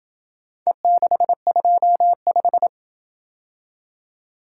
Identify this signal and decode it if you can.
Morse code, heard as a keyed tone, about 26 wpm, 700 Hz. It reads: E625